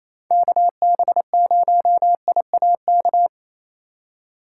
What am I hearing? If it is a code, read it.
Morse code, heard as a keyed tone, about 28 wpm, 700 Hz. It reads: KB0IAK
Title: USA Call Signs - 6 Characters - 4x Character Spacing - 28wpm